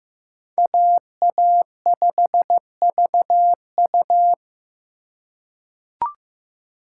Morse code, heard as a keyed tone, about 15 wpm, 700 Hz. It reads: AA5VU
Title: K1USN SST QSO Practice - 15wpm